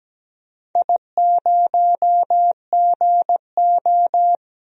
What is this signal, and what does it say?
Morse code, heard as a keyed tone, about 17 wpm, 700 Hz. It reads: I0GO
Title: DXCC-100 Call Signs - 4 Characters - 17wpm